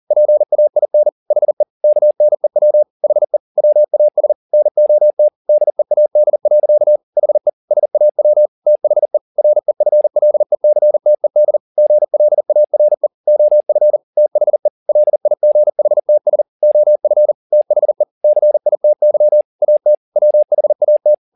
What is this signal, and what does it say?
Morse code, heard as a keyed tone, about 40 wpm, 600 Hz. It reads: PAIN HE KNEW HE WAS NOT DEAD. HE SAW THE REFLECTED GLARE OF THE LIGHTS OF THE CITY AT WHAT
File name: the old man and the sea 2 40wpm